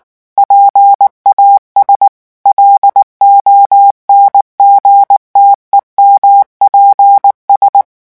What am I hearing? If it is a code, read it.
Morse code, heard as a keyed tone, about 19 wpm, 800 Hz. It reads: PAS LONGTEMPS
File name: morse mp3 / revision9 citations humoristiques vitesse 18 mots minute